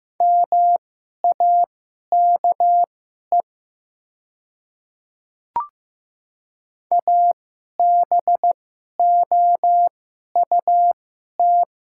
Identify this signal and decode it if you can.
Morse code, heard as a keyed tone, about 15 wpm, spaced as for 11 wpm, 700 Hz. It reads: MAKE ABOUT
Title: 2x Character Spacing - Top 200 Words - 15wpm